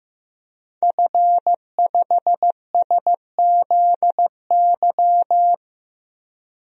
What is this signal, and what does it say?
Morse code, heard as a keyed tone, about 15 wpm, 700 Hz. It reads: F5SZY
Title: DXCC-100 Call Signs - 5 Characters - Repeated 3x - 15wpm